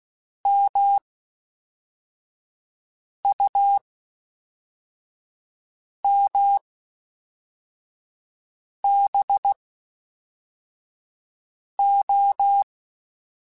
Morse code, heard as a keyed tone, about 16 wpm, spaced as for 4 wpm, 800 Hz. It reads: MUMBO
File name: Misc / Misc-mp3s / WordPairs sp @5